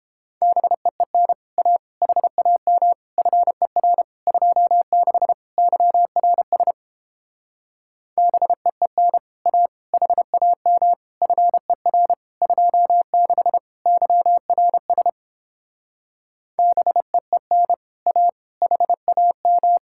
Morse code, heard as a keyed tone, about 29 wpm, 700 Hz. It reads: BEEN A HAM FER 26 YRS BEEN A HAM FER 26 YRS BEEN A HAM